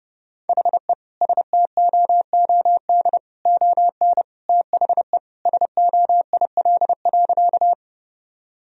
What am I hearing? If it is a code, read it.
Morse code, heard as a keyed tone, about 30 wpm, 700 Hz. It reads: HE STOOD ON THE SOIL.
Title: Sentences from Top 1000 Words - Review Part 4 - 30wpm